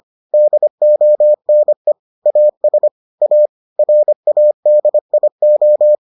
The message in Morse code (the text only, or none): DONE AS A RADIO